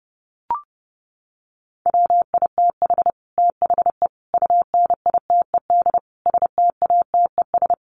Morse code, heard as a keyed tone, about 30 wpm, 700 Hz. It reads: WITH THE UNITED STATES
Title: Sets of 4 Words 30wpm